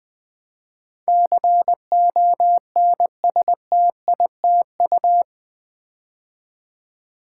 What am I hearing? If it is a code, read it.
Morse code, heard as a keyed tone, about 20 wpm, 700 Hz. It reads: CONSTITU
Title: Build A Word - 4 Letters At A Time - 12-Letter Words - 20wpm